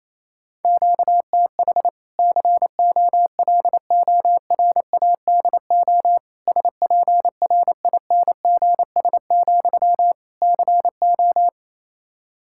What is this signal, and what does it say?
Morse code, heard as a keyed tone, about 28 wpm, 700 Hz. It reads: QTH COLORADO SPRINGS, CO
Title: Practice QSO - Exchange 1 - Beginning - 28wpm